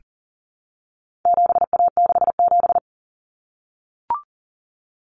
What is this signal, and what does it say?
Morse code, heard as a keyed tone, about 40 wpm, 700 Hz. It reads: 7A67
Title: Common Cut Numbers - 4-Digit 40wpm